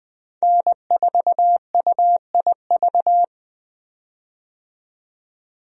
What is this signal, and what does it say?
Morse code, heard as a keyed tone, about 20 wpm, 700 Hz. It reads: N4UIV
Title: USA Call Signs - 5 Characters - 20wpm